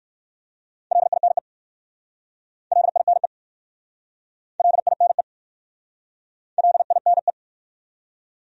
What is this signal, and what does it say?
Morse code, heard as a keyed tone, about 59 wpm, 700 Hz. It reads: LINE LINE LINE LINE